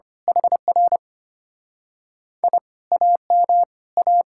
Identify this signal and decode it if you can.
Morse code, heard as a keyed tone, about 24 wpm, 700 Hz. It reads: HR I AM A
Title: Speed-Racing Common QSO Phrases - 25wpm